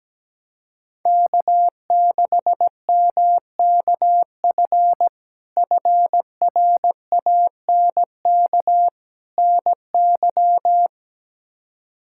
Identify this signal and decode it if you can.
Morse code, heard as a keyed tone, about 17 wpm, 700 Hz. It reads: K6MKF FRANK NY
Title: K1USN SST QSO Practice - 17wpm